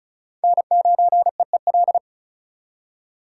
Morse code, heard as a keyed tone, about 47 wpm, spaced as for 41 wpm, 700 Hz. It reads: N9EEL
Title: Speed Racing - USA 1x3 Call Signs 35wpm